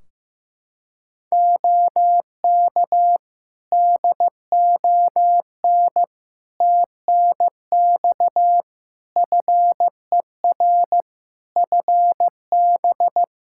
Morse code, heard as a keyed tone, about 15 wpm, 700 Hz. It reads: OK DON TNX FER FB
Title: Practice QSO - Exchange 4 - Ending - 15wpm